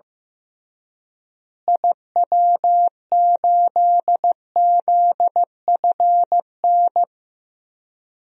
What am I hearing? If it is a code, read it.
Morse code, heard as a keyed tone, about 15 wpm, 700 Hz. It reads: IW8ZFN
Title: DXCC-100 Call Signs - 6 Characters - Repeated 3x - 15wpm